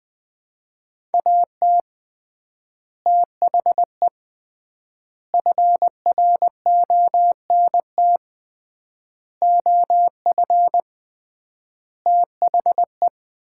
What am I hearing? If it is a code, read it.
Morse code, heard as a keyed tone, about 20 wpm, 700 Hz. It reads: AT THE FRONT OF THE
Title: 3x Word-Spacing - Sets of 5 Words - 20wpm